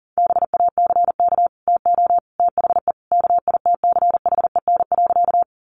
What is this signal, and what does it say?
Morse code, heard as a keyed tone, about 40 wpm, 700 Hz. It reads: BACK TO THE KITCHEN.
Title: Sentences from Top 1400 Words 40wpm